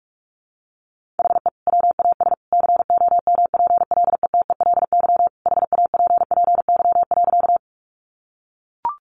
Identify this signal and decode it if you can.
Morse code, heard as a keyed tone, about 44 wpm, 700 Hz. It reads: HE WAS COMPLETELY HAPPY.
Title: Sentences from Top 1100 Words 45wpm